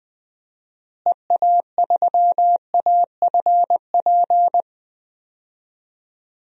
Morse code, heard as a keyed tone, about 19 wpm, 700 Hz. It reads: EA3AFP
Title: DXCC-100 Call Signs - 6 Characters - Repeated 2x - 20wpm